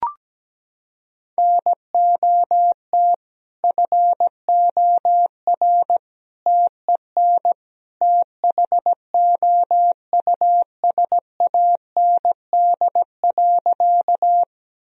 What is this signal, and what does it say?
Morse code, heard as a keyed tone, about 17 wpm, 700 Hz. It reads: NOT FOR TEN THOUSAND.
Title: Sentences from Top 1000 Words - Review Part 3 - 17wpm